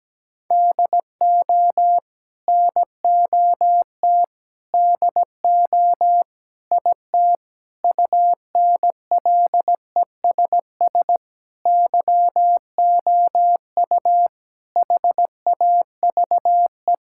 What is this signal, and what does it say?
Morse code, heard as a keyed tone, about 17 wpm, 700 Hz. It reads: DO NOT DO IT UNLESS YOU HAVE
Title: Sentences from Top 1000 Words - Review Part 6 - 17wpm